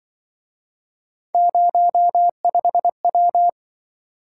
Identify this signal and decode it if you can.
Morse code, heard as a keyed tone, about 25 wpm, 700 Hz. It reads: 05W
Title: Speed-Racing - Three Letter-Number Combinations - 20wpm